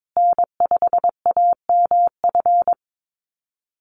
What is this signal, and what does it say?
Morse code, heard as a keyed tone, about 22 wpm, 700 Hz. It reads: N5AMF
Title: USA Call Signs Repeated 3x 22wpm